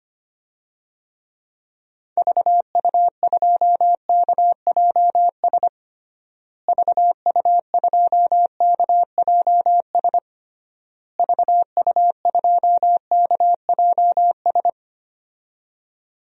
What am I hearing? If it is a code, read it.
Morse code, heard as a keyed tone, about 25 wpm, 700 Hz. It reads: VU2KJS VU2KJS VU2KJS
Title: DXCC-100 Call Signs - 6 Characters - Repeated 3x - 25wpm